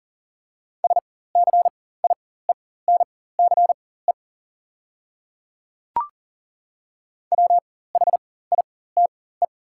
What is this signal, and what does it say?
Morse code, heard as a keyed tone, about 40 wpm, spaced as for 19 wpm, 700 Hz. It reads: SCIENCE WHITE
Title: 4x Character Spacing - Top 500 Words - 40wpm